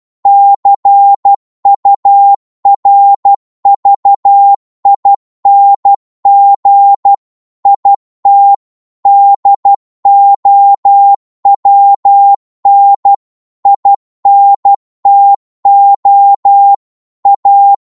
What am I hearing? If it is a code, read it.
Morse code, heard as a keyed tone, about 12 wpm, 800 Hz. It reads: CURVING IT DOWN INTO A